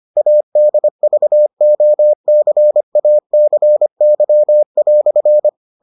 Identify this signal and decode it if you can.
Morse code, heard as a keyed tone, about 25 wpm, 600 Hz. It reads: ADVOCACY"